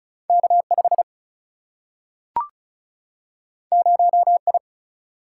Morse code, heard as a keyed tone, about 36 wpm, 700 Hz. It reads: K 5 0 I
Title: Single-Letter Single-Number 35wpm